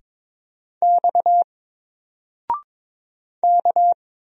Morse code, heard as a keyed tone, about 22 wpm, 700 Hz. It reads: X K
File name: Single-Letter-Number-22wpm